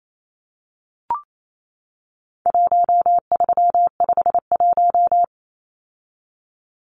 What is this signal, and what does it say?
Morse code, heard as a keyed tone, about 28 wpm, 700 Hz. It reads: 1351
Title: Numbers - 4-Digit 28wpm